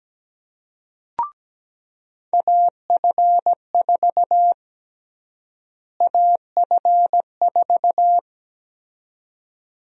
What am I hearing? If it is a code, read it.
Morse code, heard as a keyed tone, about 17 wpm, 700 Hz. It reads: AF4 AF4
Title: USA Call Sign Prefixes - Repeated 2x - 17wpm